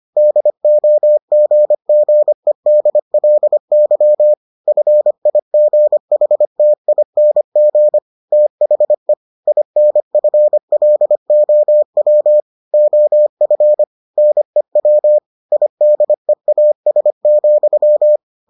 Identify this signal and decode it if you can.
Morse code, heard as a keyed tone, about 25 wpm, 600 Hz. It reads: DOGGEDLY FIGHTING THE INFLOW OF NEW IDEAS,